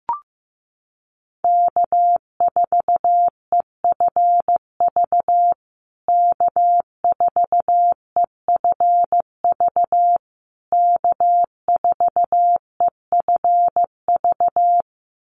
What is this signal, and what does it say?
Morse code, heard as a keyed tone, about 15 wpm, 700 Hz. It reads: K4EFV K4EFV K4EFV
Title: USA Call Signs Repeated 3x 15wpm